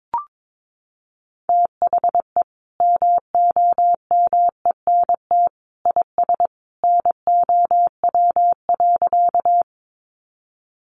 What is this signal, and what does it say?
Morse code, heard as a keyed tone, about 22 wpm, 700 Hz. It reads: THE MOMENT IS NOW.
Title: Sentences from Top 1000 Words 22wpm